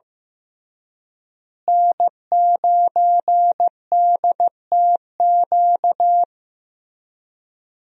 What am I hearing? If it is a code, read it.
Morse code, heard as a keyed tone, about 15 wpm, 700 Hz. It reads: N9DTQ
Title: USA Call Signs - 5 Characters - Repeated 3x - 15wpm